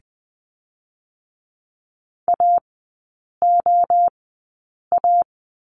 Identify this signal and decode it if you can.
Morse code, heard as a keyed tone, about 20 wpm, 700 Hz. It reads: A O A
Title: 023 Lesson 5e - Letter o - Three Characters 20wpm